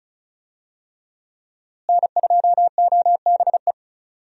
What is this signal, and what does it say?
Morse code, heard as a keyed tone, about 34 wpm, 700 Hz. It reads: N2OBE
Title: USA Call Signs - 5 Characters - Repeated 3x - 35wpm